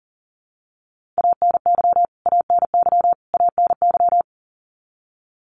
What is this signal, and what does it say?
Morse code, heard as a keyed tone, about 37 wpm, 700 Hz. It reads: ANY ANY ANY
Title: Top 500 Words Repeated 40wpm